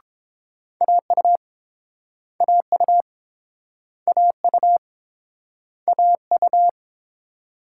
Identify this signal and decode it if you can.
Morse code, heard as a keyed tone, about 31 wpm, 700 Hz. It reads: AU AU AU AU